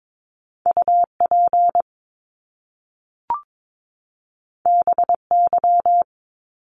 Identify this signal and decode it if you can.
Morse code, heard as a keyed tone, about 22 wpm, 700 Hz. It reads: UP BY
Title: Top Two Letter Combinations in Words 22wpm